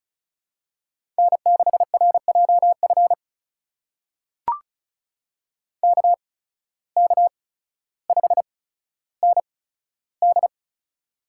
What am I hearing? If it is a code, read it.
Morse code, heard as a keyed tone, about 34 wpm, spaced as for 15 wpm, 700 Hz. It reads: N6RJF KK5ND
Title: USA Call Signs - 5 Characters - 8x Character Spacing - 35wpm